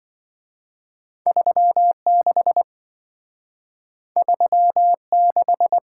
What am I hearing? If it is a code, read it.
Morse code, heard as a keyed tone, about 27 wpm, 700 Hz. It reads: 36 36